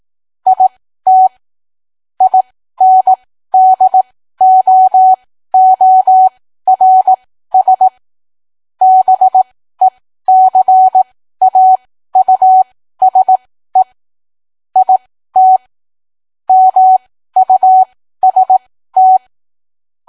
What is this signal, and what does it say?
Morse code, heard as a keyed tone, about 18 wpm, spaced as for 13 wpm, 750 Hz. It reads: IT INDOORS BECAUSE IT MUST